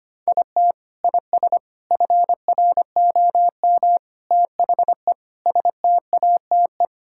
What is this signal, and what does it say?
Morse code, heard as a keyed tone, about 25 wpm, 700 Hz. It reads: IT IS FROM THE STATE
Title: Sentences from Top 1600 Words 25wpm